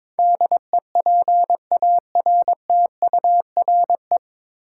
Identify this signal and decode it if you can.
Morse code, heard as a keyed tone, about 22 wpm, 700 Hz. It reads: DEPARTURE
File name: 9 Letter Words - First 3 Letters Given - 22wpm